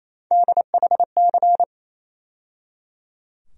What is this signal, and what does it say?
Morse code, heard as a keyed tone, about 28 wpm, 700 Hz. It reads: DHC
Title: USA Call Sign Postfixes - 3 Letters 28wpm